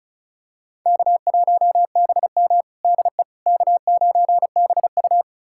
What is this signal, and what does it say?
Morse code, heard as a keyed tone, about 34 wpm, 700 Hz. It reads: K1BM DE K9BU
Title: USA Callsign DE Callsign - 4 Characters - 35wpm